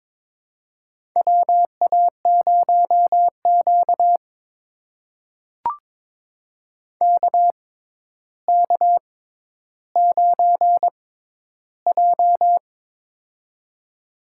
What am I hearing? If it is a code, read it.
Morse code, heard as a keyed tone, about 21 wpm, spaced as for 11 wpm, 700 Hz. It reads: WA0Q KK9J
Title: USA Call Signs - 4 Characters - 6x Character Spacing - 22wpm